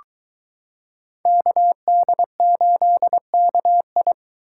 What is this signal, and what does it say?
Morse code, heard as a keyed tone, about 22 wpm, 700 Hz. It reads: KD8KI